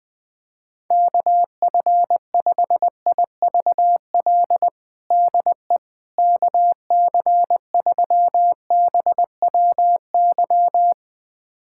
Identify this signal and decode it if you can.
Morse code, heard as a keyed tone, about 20 wpm, 700 Hz. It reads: KF5IVL DE KC3BWY
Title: USA Callsign DE Callsign - 6 Characters - 20wpm